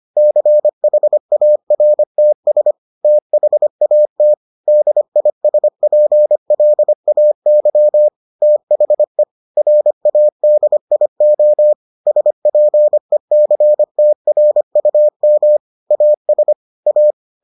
Morse code, heard as a keyed tone, about 25 wpm, 600 Hz. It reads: CHARTS THAT DISPLAY THE RADIO SPECTRUM AS A